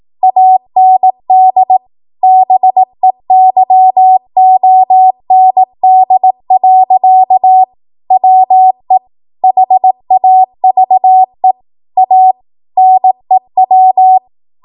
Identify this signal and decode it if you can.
Morse code, heard as a keyed tone, about 18 wpm, 750 Hz. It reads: AND BEYOND. WE HAVE A NEW
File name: Archive / 18 WPM / 18WPM